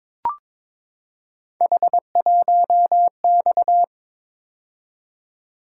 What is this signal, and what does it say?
Morse code, heard as a keyed tone, about 23 wpm, 700 Hz. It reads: H1X
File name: practice / 22wpm / Three Letter-Number Combinations - 22wpm